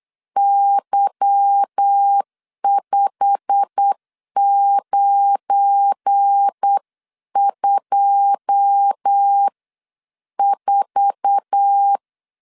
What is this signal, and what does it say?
Morse code, heard as a keyed tone, about 8 wpm, spaced as for 7 wpm, 800 Hz. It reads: Y5924
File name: CW / DET / 1 9 1500C S C 8wpm 5653